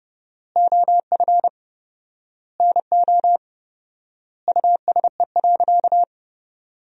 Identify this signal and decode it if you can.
Morse code, heard as a keyed tone, about 32 wpm, 700 Hz. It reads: OF NO USE.
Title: 4x Word-Spacing - Sentences from Top 100 Words - 30wpm